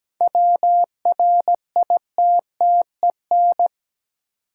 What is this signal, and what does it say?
Morse code, heard as a keyed tone, about 17 wpm, 700 Hz. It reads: WRITTEN
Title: Sentences from Top 1500 Words - New Word Drill Alt - 17wpm